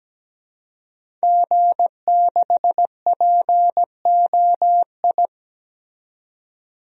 Morse code, heard as a keyed tone, about 17 wpm, 700 Hz. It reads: G6POI
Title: DXCC-100 Call Signs - 5 Characters - 17wpm